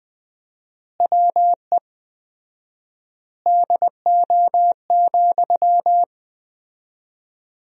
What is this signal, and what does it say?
Morse code, heard as a keyed tone, about 20 wpm, 700 Hz. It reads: WE DO,
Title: 4x Word-Spacing - Sentences from Top 100 Words - 20wpm